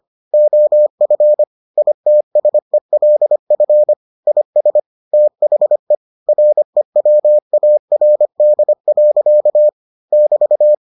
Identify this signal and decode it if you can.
Morse code, heard as a keyed tone, about 25 wpm, 600 Hz. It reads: OF ITSELF IS THE REWARD. <BT>